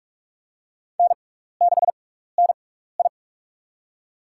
Morse code, heard as a keyed tone, about 42 wpm, spaced as for 15 wpm, 700 Hz. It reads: N6NI
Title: USA Call Signs - 4 Characters - 6x Character Spacing - 45wpm